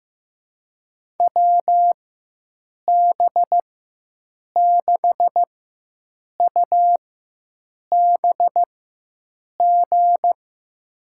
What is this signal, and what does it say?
Morse code, heard as a keyed tone, about 15 wpm, spaced as for 7 wpm, 700 Hz. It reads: WB6UBG
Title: USA Call Signs - 6 Characters - 4x Character Spacing - 15wpm